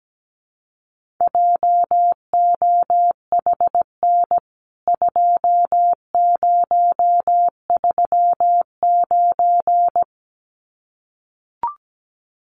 Morse code, heard as a keyed tone, about 17 wpm, 700 Hz. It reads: JOHN 2039
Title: CWT Exchange 17wpm